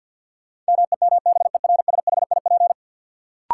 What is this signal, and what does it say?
Morse code, heard as a keyed tone, about 51 wpm, 700 Hz. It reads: MEMBERSHIP
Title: Sentences from Top 1400 Words - New Word Drill - 50wpm